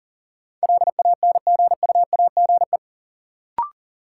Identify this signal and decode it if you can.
Morse code, heard as a keyed tone, about 40 wpm, 700 Hz. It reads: LANGUAGE
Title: Sentences from Top 500 Words - New Word Drill - 40wpm